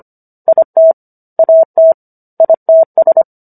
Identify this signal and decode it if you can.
Morse code, heard as a keyed tone, about 25 wpm, 650 Hz. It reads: IT AT ITS